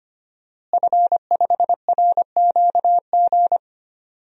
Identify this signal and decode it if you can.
Morse code, heard as a keyed tone, about 25 wpm, 700 Hz. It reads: F5RQG
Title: DXCC-100 Call Signs - 5 Characters - Repeated 3x - 25wpm